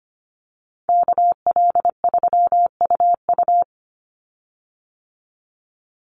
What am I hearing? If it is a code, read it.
Morse code, heard as a keyed tone, about 25 wpm, 700 Hz. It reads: KL3UU
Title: USA Call Signs 25wpm